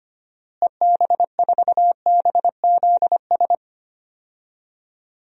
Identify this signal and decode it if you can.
Morse code, heard as a keyed tone, about 25 wpm, 700 Hz. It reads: EB4BZS
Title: DXCC-100 Call Signs - 6 Characters - Repeated 3x - 25wpm